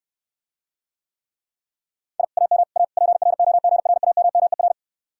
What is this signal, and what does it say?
Morse code, heard as a keyed tone, about 69 wpm, 700 Hz. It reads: I AM A PROGRAMMER